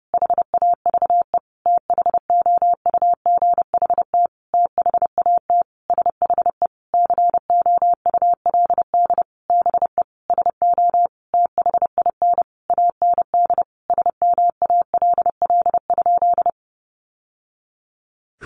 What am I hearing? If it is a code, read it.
Morse code, heard as a keyed tone, about 30 wpm, 700 Hz. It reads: HAVE THOUGHT THAT SHE COULD BE SO THIN AND SMALL?
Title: Sentences from Top 1200 Words 30wpm